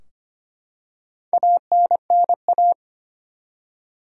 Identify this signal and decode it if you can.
Morse code, heard as a keyed tone, about 25 wpm, 700 Hz. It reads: ANNA
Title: USA Common Names - Max 4 Letters - 25wpm